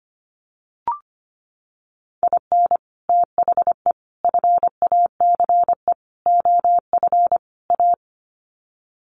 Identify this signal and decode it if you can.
Morse code, heard as a keyed tone, about 25 wpm, 700 Hz. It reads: IN THE FACE OF A
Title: Sets of 5 Words Encore 25wpm